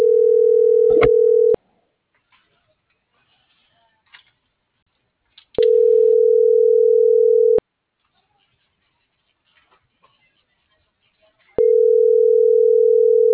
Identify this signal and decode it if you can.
Ringback tone (North America), 440 + 480 Hz.